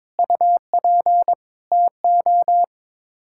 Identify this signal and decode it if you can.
Morse code, heard as a keyed tone, about 22 wpm, 700 Hz. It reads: UP TO
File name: practice / 22wpm / Sets of 6 Words - 22wpm